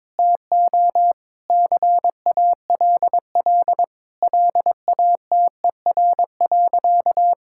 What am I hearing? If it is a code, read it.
Morse code, heard as a keyed tone, about 22 wpm, 700 Hz. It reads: TO CALL LATER.
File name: practice / 22wpm / Sentences with 4-Word Phrases - 22wpm